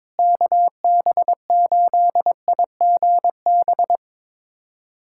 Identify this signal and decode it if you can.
Morse code, heard as a keyed tone, about 21 wpm, 700 Hz. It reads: KB8IGB
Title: USA Callsign DE Callsign - 6 Characters - 22wpm